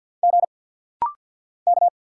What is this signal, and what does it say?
Morse code, heard as a keyed tone, about 50 wpm, 690 Hz. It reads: G K
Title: Single Letter-Number Alt - Rapid-Fire 50wpm